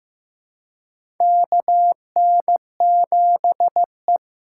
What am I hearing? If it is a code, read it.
Morse code, heard as a keyed tone, about 15 wpm, 700 Hz. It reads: KN7E
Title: USA Call Signs - 4 Characters - 3x Character Spacing - 15wpm